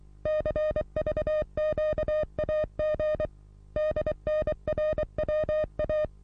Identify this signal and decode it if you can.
Morse code, heard as a keyed tone, about 23 wpm, 610 Hz. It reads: CVQAG DNRWA
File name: repos / Morsekurs / DJ4UF Morsen27